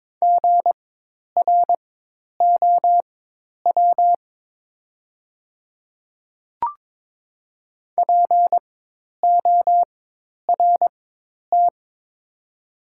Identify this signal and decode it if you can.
Morse code, heard as a keyed tone, about 22 wpm, spaced as for 10 wpm, 700 Hz. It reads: GROW PORT
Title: 4x Character Spacing - Top 300 Words - 22wpm